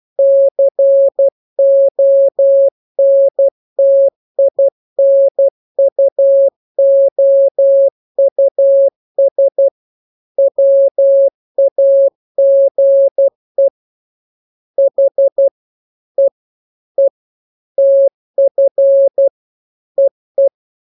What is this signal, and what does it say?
Morse code, heard as a keyed tone, about 12 wpm, 550 Hz. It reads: CONTINUOUS WAGE H E E TF EE
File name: jokes / joke 2 12 0000